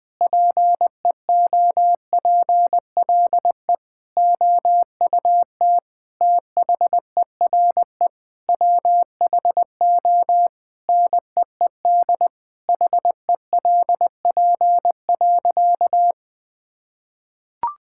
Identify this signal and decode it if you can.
Morse code, heard as a keyed tone, about 20 wpm, 700 Hz. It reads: PEOPLE OUT THERE WHO NEED HELP.
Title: Sentences with 8-Word Phrases - Spoken Beginning - 20wpm